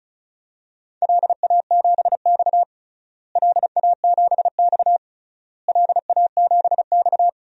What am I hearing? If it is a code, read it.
Morse code, heard as a keyed tone, about 34 wpm, 700 Hz. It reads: LA7X LA7X LA7X
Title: DXCC-100 Call Signs - 4 Characters - Repeated 3x - 35wpm